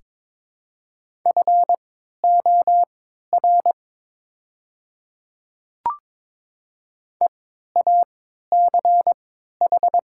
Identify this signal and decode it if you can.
Morse code, heard as a keyed tone, about 22 wpm, spaced as for 12 wpm, 700 Hz. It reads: FOR EACH
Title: 3x Character Spacing - Top 100 Words - 22wpm